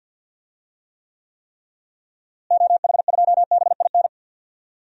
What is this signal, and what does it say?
Morse code, heard as a keyed tone, about 50 wpm, 700 Hz. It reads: OH2BIN
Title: DXCC-100 Call Signs - 6 Characters - Repeated 3x - 50wpm